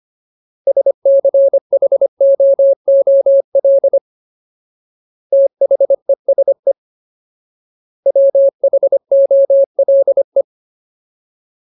Morse code, heard as a keyed tone, about 25 wpm, 550 Hz. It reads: SCHOOL THESE WHOLE
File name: Ultimate Word List / 25WPM Ultimate